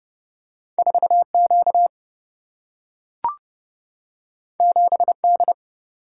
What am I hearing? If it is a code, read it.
Morse code, heard as a keyed tone, about 30 wpm, 700 Hz. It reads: 4 Q 7 D
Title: Single-Letter Single-Number 30wpm